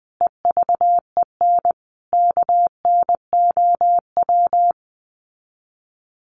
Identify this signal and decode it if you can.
Morse code, heard as a keyed tone, about 20 wpm, 700 Hz. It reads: EVEN KNOW
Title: Sets of 5 Words Encore2 20wpm